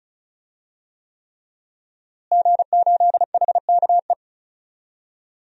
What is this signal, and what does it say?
Morse code, heard as a keyed tone, about 36 wpm, 700 Hz. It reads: G8HKE
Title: DXCC-100 Call Signs - 5 Characters - Repeated 3x - 35wpm